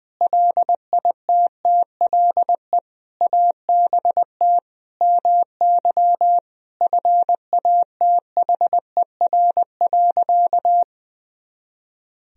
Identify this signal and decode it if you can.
Morse code, heard as a keyed tone, about 20 wpm, 700 Hz. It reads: LITTLE ABT MY FATHER.
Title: Sentences from Top 400 Words and Abbreviations 20wpm